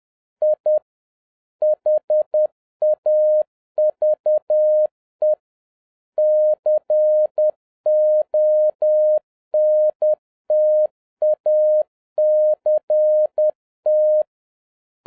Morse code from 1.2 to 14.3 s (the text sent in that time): HAVE CONTACT